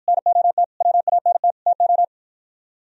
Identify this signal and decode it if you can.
Morse code, heard as a keyed tone, about 51 wpm, 700 Hz. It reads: NOT WANT TO